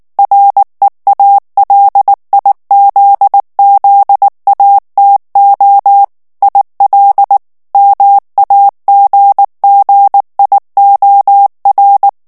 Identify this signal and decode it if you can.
Morse code, heard as a keyed tone, about 19 wpm, 800 Hz. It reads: REALIZZATO IL MAGGIOR
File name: morse mp3 / ITALIEN texte clair 2 vitesse 18 mots minute